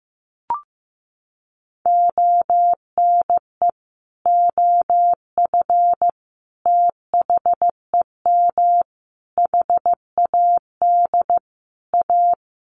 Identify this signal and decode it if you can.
Morse code, heard as a keyed tone, about 15 wpm, 700 Hz. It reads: ONE OF THEM HAD A
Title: Sentences from Top 1300 Words 15wpm